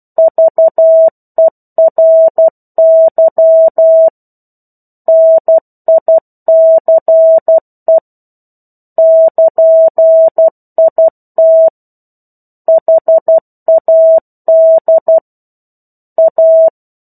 Morse code, heard as a keyed tone, about 12 wpm, 650 Hz. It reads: VERY NICE <KN>IT HAD A
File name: 01-Chapter-01 12 WPM